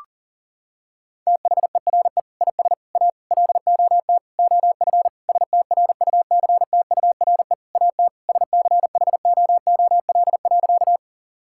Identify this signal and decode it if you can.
Morse code, heard as a keyed tone, about 40 wpm, 700 Hz. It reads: THERE IS A LOT OF STRUCTURE AT SCHOOL.
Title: Sentences from Top 1000 Words - Review Part 6 - 40wpm